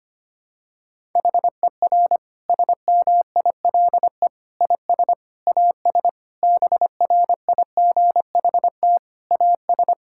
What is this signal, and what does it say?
Morse code, heard as a keyed tone, about 25 wpm, 700 Hz. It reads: HER SMILE IS AS BRIGHT AS
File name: practice / 25wpm / Difficult Sentences with Hints - 25wpm